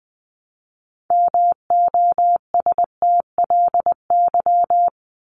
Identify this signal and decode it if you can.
Morse code, heard as a keyed tone, about 20 wpm, 700 Hz. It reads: MOSTLY
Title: Top 1001-1250 Words 20wpm